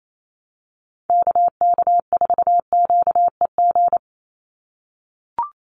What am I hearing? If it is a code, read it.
Morse code, heard as a keyed tone, about 28 wpm, 700 Hz. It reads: KK4QEG
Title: USA Call Signs 28wpm